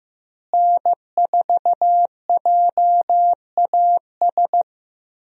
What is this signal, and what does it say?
Morse code, heard as a keyed tone, about 15 wpm, 700 Hz. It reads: N4JAS
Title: USA Call Signs - 5 Characters - Repeated 3x - 15wpm